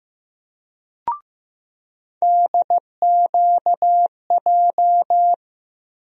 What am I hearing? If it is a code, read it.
Morse code, heard as a keyed tone, about 15 wpm, 700 Hz. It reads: DQJ
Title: USA Call Sign Postfixes - 3 Letters 15wpm